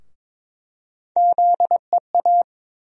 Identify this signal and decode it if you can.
Morse code, heard as a keyed tone, about 22 wpm, 700 Hz. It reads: ZEA
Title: USA Call Sign Postfixes - 3 Letters 22wpm